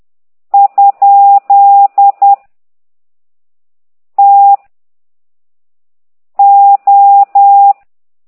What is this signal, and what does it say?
Morse code, heard as a keyed tone, about 10 wpm, 800 Hz. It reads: ? T O